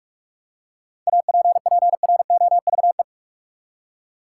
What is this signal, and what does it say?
Morse code, heard as a keyed tone, about 46 wpm, 700 Hz. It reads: APPROVE
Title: Seven Letter Words - 45wpm